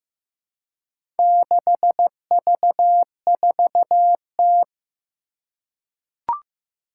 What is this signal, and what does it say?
Morse code, heard as a keyed tone, about 15 wpm, 700 Hz. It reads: 6V4T
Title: Four Challenging Letter-Number Combinations 15wpm